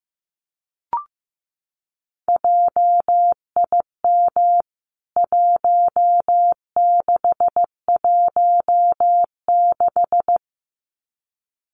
Morse code, heard as a keyed tone, about 15 wpm, 700 Hz. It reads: JIM 1616
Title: CWT Exchange 15wpm